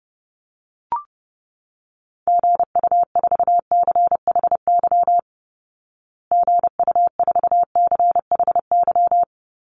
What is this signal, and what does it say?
Morse code, heard as a keyed tone, about 30 wpm, 700 Hz. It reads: GU4CHY GU4CHY
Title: Call Signs Repeated 3x - 30wpm